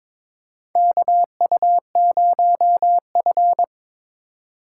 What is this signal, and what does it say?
Morse code, heard as a keyed tone, about 22 wpm, 700 Hz. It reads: KU0F
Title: USA Call Signs - 4 Characters - 8x Character Spacing - 22wpm